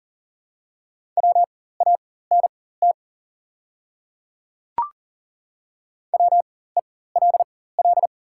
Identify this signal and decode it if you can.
Morse code, heard as a keyed tone, about 40 wpm, spaced as for 19 wpm, 700 Hz. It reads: WANT WELL
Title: 4x Character Spacing - Top 200 Words - 40wpm